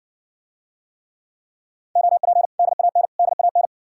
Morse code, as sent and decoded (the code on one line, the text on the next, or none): --- .--. / -.. .- -. / -.. .- -.
OP DAN DAN